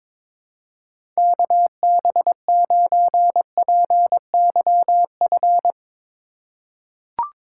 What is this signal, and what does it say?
Morse code, heard as a keyed tone, about 22 wpm, 700 Hz. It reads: KB9PYF
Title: USA Call Signs - 6 Characters - 3x Character Spacing - 22wpm